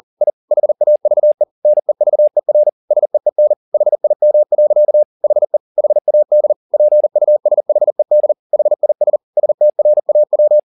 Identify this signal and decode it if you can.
Morse code, heard as a keyed tone, about 40 wpm, 600 Hz. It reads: I HAVE NEVER SEEN HIM. HE HAD PUSHED HIS STRAW